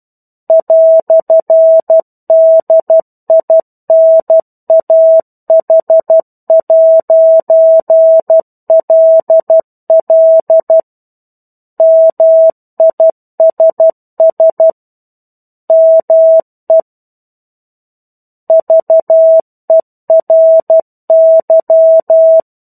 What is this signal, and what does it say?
Morse code, heard as a keyed tone, about 12 wpm, 650 Hz. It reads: "DINAH'LL MISS ME VERY
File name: 01-Chapter-01 12 WPM